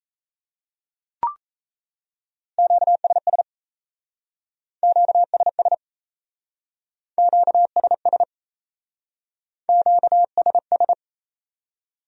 Morse code, heard as a keyed tone, about 37 wpm, 700 Hz. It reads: QSS QSS QSS QSS